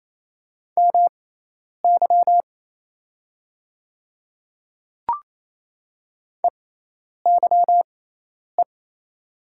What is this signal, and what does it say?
Morse code, heard as a keyed tone, about 28 wpm, spaced as for 10 wpm, 690 Hz. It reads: MY EYE
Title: 6x Character Spacing - Top 300 Words - 28wpm